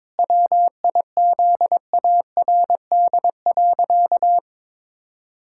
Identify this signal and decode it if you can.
Morse code, heard as a keyed tone, about 22 wpm, 700 Hz. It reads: WIZARD.
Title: Pangram Challenge v2 - 22wpm